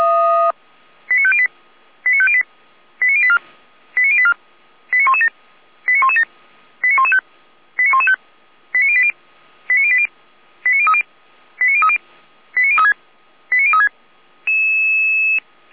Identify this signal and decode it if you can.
Five-tone selective call (ZVEI), heard as quick runs of five tones, tones of about 70 ms. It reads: 89508 89508 89084 89084 89117 89117 89115 89115 89080 89080 89030 89030 89036 89036
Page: /455/SDR/ZVEI/89120/